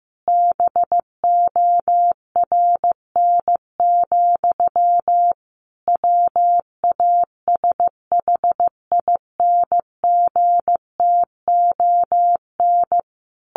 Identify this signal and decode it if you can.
Morse code, heard as a keyed tone, about 15 wpm, 700 Hz. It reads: BORN, WASHINGTON
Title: A Sound Of Thunder 15wpm